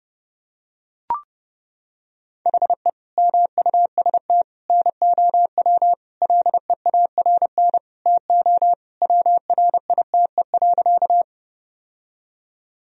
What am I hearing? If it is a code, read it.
Morse code, heard as a keyed tone, about 32 wpm, 700 Hz. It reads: HE MUST NOW LEARN TO WRITE.
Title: Sentences from Top 1000 Words - Review Part 1 - 30wpm